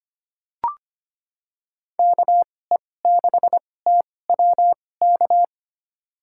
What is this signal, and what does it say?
Morse code, heard as a keyed tone, about 25 wpm, spaced as for 18 wpm, 700 Hz. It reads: KE6TWK
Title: USA Call Signs - 6 Characters - 2x Character Spacing - 25wpm